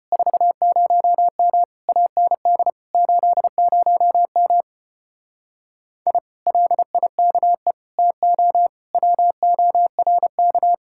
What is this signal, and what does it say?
Morse code, heard as a keyed tone, about 33 wpm, 700 Hz. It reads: 40M AND 80M I LIKE TO WORK